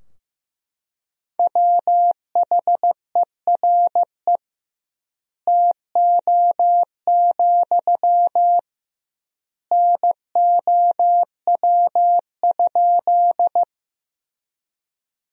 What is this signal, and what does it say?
Morse code, heard as a keyed tone, about 14 wpm, 700 Hz. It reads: WHERE TO, NOW?
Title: 2x Word-Spacing - Sentences from Top 200 Words - 15wpm